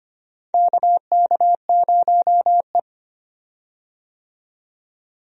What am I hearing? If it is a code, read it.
Morse code, heard as a keyed tone, about 25 wpm, 700 Hz. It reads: KK0E
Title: USA Call Signs - 4 Characters - 25wpm